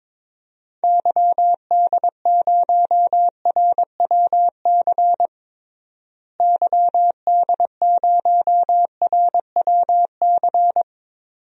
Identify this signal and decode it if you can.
Morse code, heard as a keyed tone, about 21 wpm, 700 Hz. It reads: YD0RWC YD0RWC
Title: DXCC-100 Call Signs - 6 Characters - Repeated 2x - 22wpm